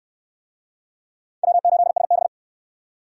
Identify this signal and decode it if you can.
Morse code, heard as a keyed tone, about 60 wpm, 700 Hz. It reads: W8SB